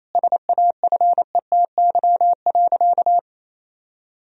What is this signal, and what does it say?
Morse code, heard as a keyed tone, about 28 wpm, 700 Hz. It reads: SAFETY.
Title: Sentences from Top 1000 Words - Review Part 4 - 28wpm